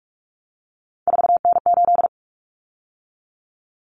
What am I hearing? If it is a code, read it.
Morse code, heard as a keyed tone, about 44 wpm, 700 Hz. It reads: 4N8
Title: Common Cut Numbers - 3-Digit 45wpm